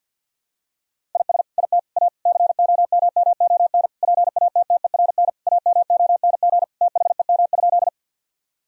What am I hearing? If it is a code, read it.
Morse code, heard as a keyed tone, about 52 wpm, 700 Hz. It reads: IS IT A COMMON PATTERN AMONG THEM?
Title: Sentences from Top 1000 Words - Review Part 2 - 50wpm